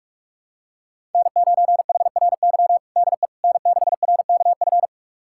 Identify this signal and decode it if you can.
Morse code, heard as a keyed tone, about 41 wpm, 700 Hz. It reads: N9HRY DE N6RKF
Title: USA Callsign DE Callsign - 5 Characters - 45wpm